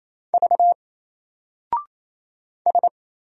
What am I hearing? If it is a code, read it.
Morse code, heard as a keyed tone, about 27 wpm, 700 Hz. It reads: V S
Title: Single Character (Letter:Number:Punctuation:Prosign) 28wpm